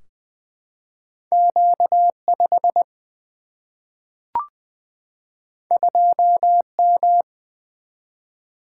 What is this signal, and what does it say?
Morse code, heard as a keyed tone, about 20 wpm, 700 Hz. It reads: Q 5 2 M
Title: Single-Letter Single-Number 20wpm